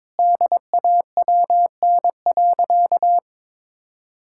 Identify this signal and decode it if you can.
Morse code, heard as a keyed tone, about 22 wpm, 700 Hz. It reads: DAWN.
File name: Difficult Sentences with Hints - 22wpm